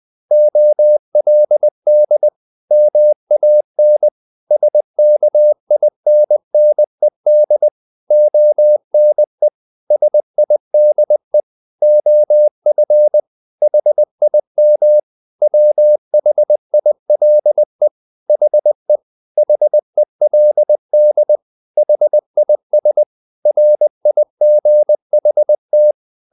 Morse code, heard as a keyed tone, about 20 wpm, 600 Hz. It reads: OLD MAN SKINNED ONE SIDE OF HIM WHILE HE HELD HIS RIGHT
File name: the old man and the sea part 2 20wpm